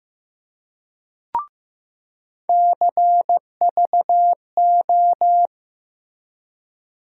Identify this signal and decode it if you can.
Morse code, heard as a keyed tone, about 15 wpm, 700 Hz. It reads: CVO
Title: USA Call Sign Postfixes - 3 Letters 15wpm